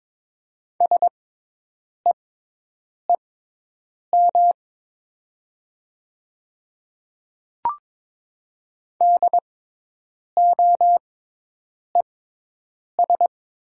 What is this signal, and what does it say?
Morse code, heard as a keyed tone, about 22 wpm, spaced as for 8 wpm, 700 Hz. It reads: SEEM DOES
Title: 6x Character Spacing - Top 300 Words - 22wpm